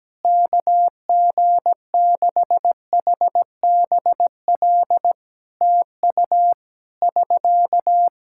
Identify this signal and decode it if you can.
Morse code, heard as a keyed tone, about 17 wpm, 700 Hz. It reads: KG6HBL TU <SK>
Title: Practice QSO - Exchange 4 - Ending - 17wpm